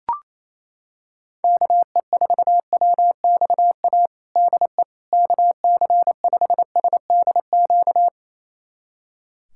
Morse code, heard as a keyed tone, about 28 wpm, 700 Hz. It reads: KE4WXA DE KC5SDQ
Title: USA Callsign DE Callsign - 6 Characters - 28wpm